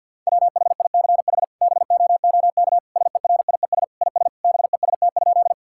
Morse code, heard as a keyed tone, about 50 wpm, 700 Hz. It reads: WHICH BOOK SERIES IS BEST?
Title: Sentences from Top 1000 Words - Review Part 5 - 50wpm